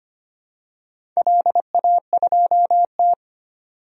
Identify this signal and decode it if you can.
Morse code, heard as a keyed tone, about 25 wpm, 700 Hz. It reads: LA2T
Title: DXCC-100 Call Signs - 4 Characters - 25wpm